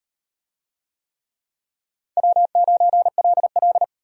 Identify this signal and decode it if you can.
Morse code, heard as a keyed tone, about 38 wpm, 700 Hz. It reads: W9LL